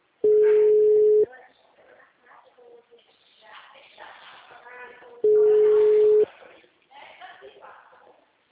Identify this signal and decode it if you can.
Ringback tone (Europe, 425 Hz), with speech.